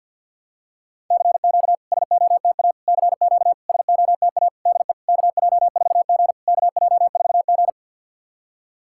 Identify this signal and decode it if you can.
Morse code, heard as a keyed tone, about 51 wpm, 690 Hz. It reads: CQ SOTA CQ SOTA DE KJ4G KJ4G
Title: SOTA QSO Practice 50wpm